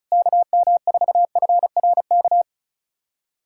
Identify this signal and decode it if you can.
Morse code, heard as a keyed tone, about 34 wpm, 700 Hz. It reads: KM4FRK
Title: USA Callsign DE Callsign 35wpm